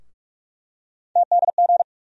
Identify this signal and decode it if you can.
Morse code, heard as a keyed tone, about 47 wpm, 700 Hz. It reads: TDG
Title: USA Call Sign Postfixes - 3 Letters 45wpm